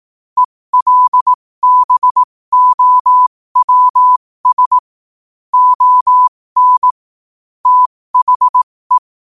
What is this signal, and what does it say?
Morse code, heard as a keyed tone, about 18 wpm, spaced as for 15 wpm, 1000 Hz. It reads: ELBOWS ON THE
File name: qotdfiles / 15wpm